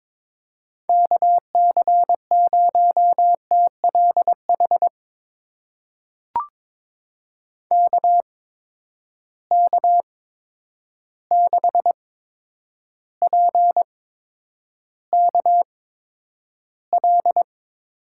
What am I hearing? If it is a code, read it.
Morse code, heard as a keyed tone, about 22 wpm, spaced as for 9 wpm, 700 Hz. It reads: KC0TLH KK6PKL
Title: USA Call Signs - 6 Characters - 8x Character Spacing - 22wpm